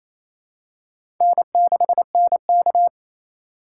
Morse code, heard as a keyed tone, about 28 wpm, 700 Hz. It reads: N6NK
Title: USA Call Signs - 3x Character Spacing - 28wpm